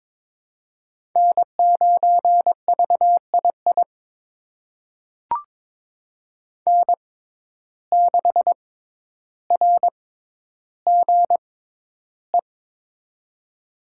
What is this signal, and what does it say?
Morse code, heard as a keyed tone, about 22 wpm, spaced as for 11 wpm, 700 Hz. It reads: N9VII N6RGE
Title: USA Call Signs - 5 Characters - 6x Character Spacing - 22wpm